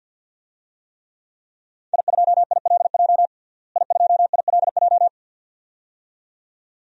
Morse code, heard as a keyed tone, about 51 wpm, 700 Hz. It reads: I2ILJ I2ILJ
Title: DXCC-100 Call Signs - 5 Characters - Repeated 2x - 50wpm